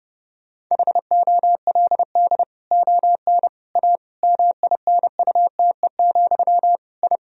Morse code, heard as a keyed tone, about 32 wpm, 700 Hz. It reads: HOLD ON A MINUTE, I
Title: Sentences from Top 1000 Words - Review Part 2 - 30wpm